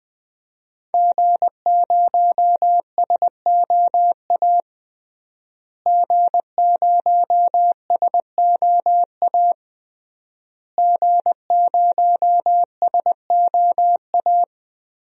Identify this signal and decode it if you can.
Morse code, heard as a keyed tone, about 20 wpm, 700 Hz. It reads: G0SOA G0SOA G0SOA
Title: DXCC-100 Call Signs - 5 Characters - Repeated 3x - 20wpm